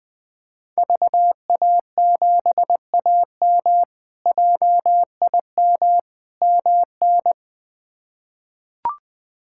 Morse code, heard as a keyed tone, about 21 wpm, 700 Hz. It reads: VA7AM JIM MN
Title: K1USN SST QSO Practice - 20wpm